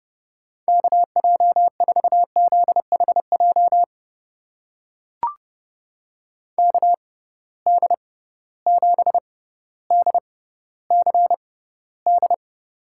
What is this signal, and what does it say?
Morse code, heard as a keyed tone, about 30 wpm, spaced as for 15 wpm, 700 Hz. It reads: KJ4ZHJ KD7DCD
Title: USA Call Signs - 6 Characters - 6x Character Spacing - 30wpm